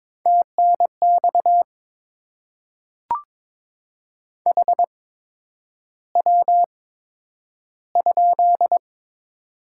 Morse code, heard as a keyed tone, about 22 wpm, spaced as for 9 wpm, 700 Hz. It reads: TNX HW?
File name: Top 20 QSO Elements - 8x Character Spacing - 22wpm